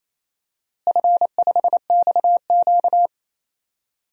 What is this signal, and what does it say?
Morse code, heard as a keyed tone, about 28 wpm, 700 Hz. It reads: F5XQ
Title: DXCC-100 Call Signs - 4 Characters - Repeated 3x - 28wpm